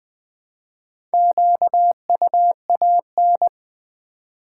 Morse code, heard as a keyed tone, about 20 wpm, 700 Hz. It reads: QUAN
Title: Build A Word - 2 Letters At A Time - 8-Letter Words - 20wpm